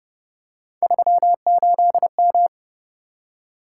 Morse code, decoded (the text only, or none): 38M